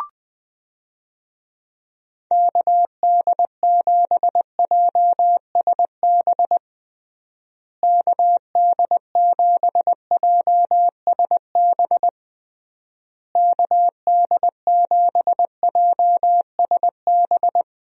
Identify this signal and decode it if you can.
Morse code, heard as a keyed tone, about 20 wpm, 700 Hz. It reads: KD7JSB KD7JSB KD7JSB
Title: USA Call Signs - 6 Characters - Repeated 3x - 20wpm